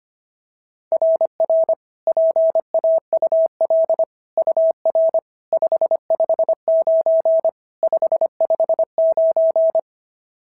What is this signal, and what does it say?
Morse code, heard as a keyed tone, about 25 wpm, 690 Hz. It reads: RR PAUL UR 559 559
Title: SOTA QSO Practice 25wpm